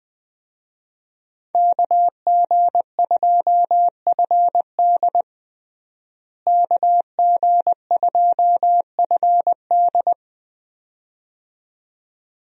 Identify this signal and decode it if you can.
Morse code, heard as a keyed tone, about 20 wpm, 700 Hz. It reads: KG2FD KG2FD
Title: USA Call Signs - 5 Characters - Repeated 2x - 20wpm